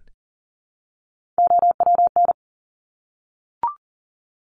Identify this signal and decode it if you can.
Morse code, heard as a keyed tone, about 40 wpm, 700 Hz. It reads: OWN
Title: Top Three Letter Combinations in Words 40wpm